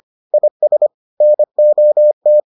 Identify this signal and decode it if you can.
Morse code, heard as a keyed tone, about 25 wpm, 600 Hz. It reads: IS NOT